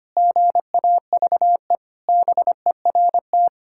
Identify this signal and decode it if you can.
Morse code, heard as a keyed tone, about 25 wpm, 700 Hz. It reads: GAVE BERT
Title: Pangram Challenge v2 - 25wpm